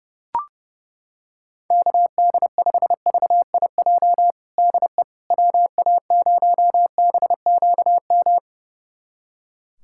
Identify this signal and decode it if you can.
Morse code, heard as a keyed tone, about 32 wpm, 690 Hz. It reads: KD5VIJ DE WA0BQM
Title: USA Callsign DE Callsign - 6 Characters - 30wpm